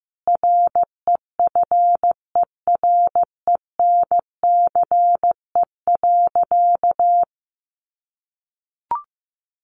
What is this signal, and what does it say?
Morse code, heard as a keyed tone, about 15 wpm, 700 Hz. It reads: REFERENCE.
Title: Sentences from Top 1200 Words 15wpm